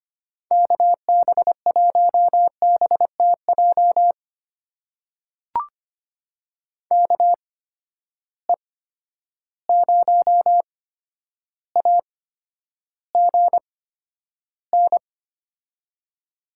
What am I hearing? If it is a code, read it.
Morse code, heard as a keyed tone, about 25 wpm, spaced as for 11 wpm, 700 Hz. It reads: KB1BTJ KE0AGN
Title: USA Call Signs - 6 Characters - 8x Character Spacing - 25wpm